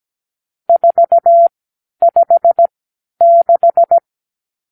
Morse code, heard as a keyed tone, about 17 wpm, spaced as for 10 wpm, 680 Hz. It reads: 456
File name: k7qo / Lesson 50 (1-6)